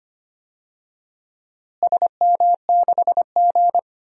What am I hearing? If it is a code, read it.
Morse code, heard as a keyed tone, about 25 wpm, 700 Hz. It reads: SM6G
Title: DXCC-100 Call Signs - 4 Characters - Repeated 2x - 25wpm